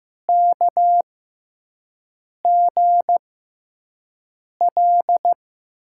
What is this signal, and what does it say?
Morse code, heard as a keyed tone, about 15 wpm, spaced as for 5 wpm, 700 Hz. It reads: KGL
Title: USA Call Signs - 6 Characters - 6x Character Spacing - 15wpm